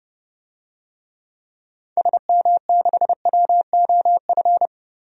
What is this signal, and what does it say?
Morse code, heard as a keyed tone, about 30 wpm, 700 Hz. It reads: SM6WOF
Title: DXCC-100 Call Signs - 6 Characters - 30wpm